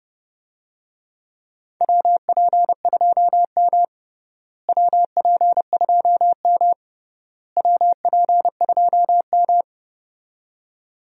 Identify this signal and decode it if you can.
Morse code, heard as a keyed tone, about 32 wpm, 700 Hz. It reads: WP2M WP2M WP2M
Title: USA Call Signs - 4 Characters - Repeated 3x - 30wpm